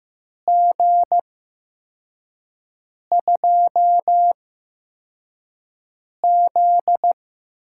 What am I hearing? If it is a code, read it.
Morse code, heard as a keyed tone, about 15 wpm, spaced as for 4 wpm, 700 Hz. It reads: G2Z
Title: USA Call Signs - 8x Character Spacing - 15wpm